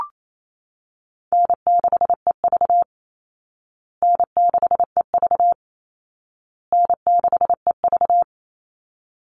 Morse code, heard as a keyed tone, about 28 wpm, 700 Hz. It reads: N6EV N6EV N6EV
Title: Call Signs Repeated 2x - 28wpm